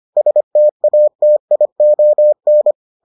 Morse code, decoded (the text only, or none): STATION